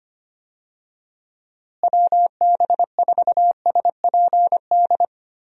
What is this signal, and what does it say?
Morse code, heard as a keyed tone, about 25 wpm, 700 Hz. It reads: WB4SPD
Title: USA Call Signs - 6 Characters - Repeated 2x - 25wpm